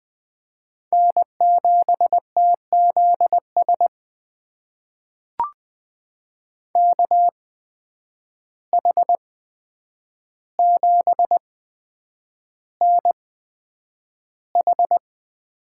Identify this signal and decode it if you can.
Morse code, heard as a keyed tone, about 20 wpm, spaced as for 9 wpm, 700 Hz. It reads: N7TZS KH7NH
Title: USA Call Signs - 5 Characters - 8x Character Spacing - 20wpm